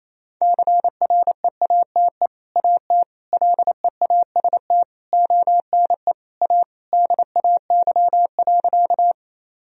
Morse code, heard as a keyed tone, about 28 wpm, 700 Hz. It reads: CREATE AT LEAST ONE A DAY.
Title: Sentences from Top 1000 Words - Review Part 3 - 28wpm